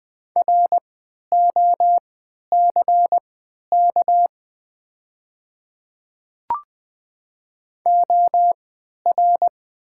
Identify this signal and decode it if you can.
Morse code, heard as a keyed tone, about 20 wpm, spaced as for 11 wpm, 700 Hz. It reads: ROCK OR
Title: 3x Character Spacing - Top 500 Words - 20wpm